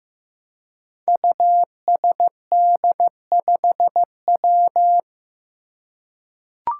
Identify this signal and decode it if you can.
Morse code, heard as a keyed tone, about 15 wpm, 700 Hz. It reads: USD5W
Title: Five Challenging Letter-Number Combinations 15wpm